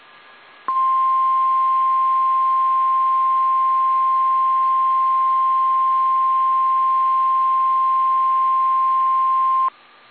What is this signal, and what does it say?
Emergency Alert System (SAME) tones (the 1050 Hz attention tone). It is heard from 0.6 to 9.7 s.
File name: eas / KSDZKDJL / KSDZKDJL